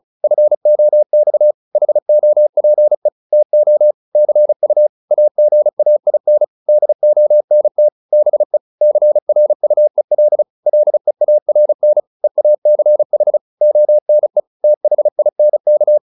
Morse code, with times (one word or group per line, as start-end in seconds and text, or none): 0.2-1.5 FOX
1.8-3.1 HOPE
3.3-3.9 TO
4.2-4.9 CU
5.1-6.4 AGAIN
6.7-7.9 DONT
8.1-8.6 BE
8.8-10.4 CRUEL
10.7-12.0 LEARN
12.2-13.4 EACH
13.6-14.4 ONE
14.6-16.0 THINK